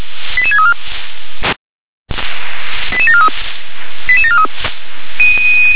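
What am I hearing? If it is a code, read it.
Five-tone selective call (ZVEI), heard as quick runs of five tones, tones of about 70 ms. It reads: 88634 88743 88743
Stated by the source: ZVEI five-tone selective call received on an SDR.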